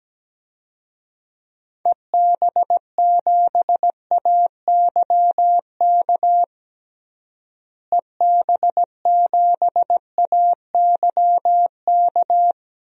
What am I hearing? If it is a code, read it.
Morse code, heard as a keyed tone, about 17 wpm, 700 Hz. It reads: EB7AYK EB7AYK
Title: DXCC-100 Call Signs - 6 Characters - Repeated 3x - 17wpm